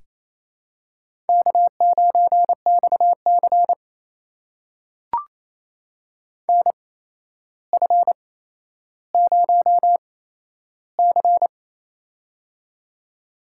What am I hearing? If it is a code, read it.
Morse code, heard as a keyed tone, about 28 wpm, spaced as for 12 wpm, 700 Hz. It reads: K9XC NF0C
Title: USA Call Signs - 4 Characters - 8x Character Spacing - 28wpm